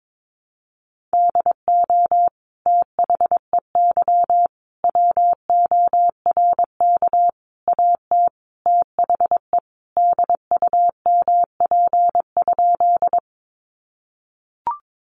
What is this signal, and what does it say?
Morse code, heard as a keyed tone, about 22 wpm, 700 Hz. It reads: DO THEY WORK AT THE DUMP?
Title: Sentences from Top 1300 Words 22wpm